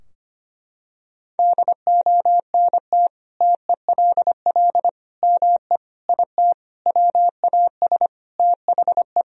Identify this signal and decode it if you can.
Morse code, heard as a keyed tone, about 25 wpm, 700 Hz. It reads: DONT TELL ME IT WAS THE
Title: Sentences from Top 1000 Words - Review Part 3 - 25wpm